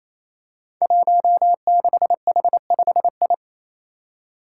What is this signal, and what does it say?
Morse code, heard as a keyed tone, about 28 wpm, 700 Hz. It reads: 16H5I
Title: Five Challenging Letter-Number Combinations 28wpm